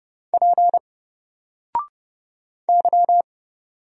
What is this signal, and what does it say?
Morse code, heard as a keyed tone, about 31 wpm, 700 Hz. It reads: P Y
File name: Single-Letter-Number-30wpm